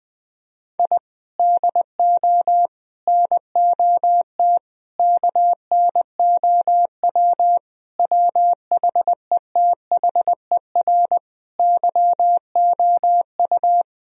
Morse code, heard as a keyed tone, about 20 wpm, 700 Hz. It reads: I DO NOT KNOW WHETHER YOU
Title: Sentences from Top 1000 Words - Review Part 3 - 20wpm